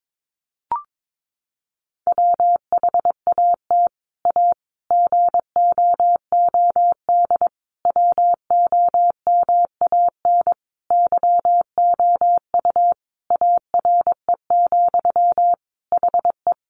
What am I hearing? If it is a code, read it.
Morse code, heard as a keyed tone, about 22 wpm, 700 Hz. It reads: WHAT A GOOD WOMAN YOU ARE, HE
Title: Sentences from Top 1100 Words 22wpm